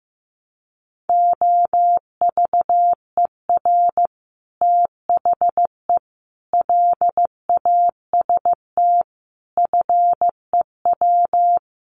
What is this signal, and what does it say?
Morse code, heard as a keyed tone, about 15 wpm, 700 Hz. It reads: OVER THE LAST FEW
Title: Sets of 5 Words Encore 15wpm